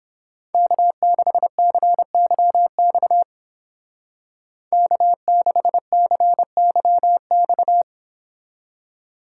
Morse code, heard as a keyed tone, about 30 wpm, 700 Hz. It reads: K6CYX K6CYX